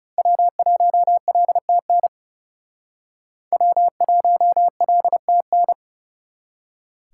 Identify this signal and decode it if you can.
Morse code, heard as a keyed tone, about 38 wpm, 700 Hz. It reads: W1LTN W1LTN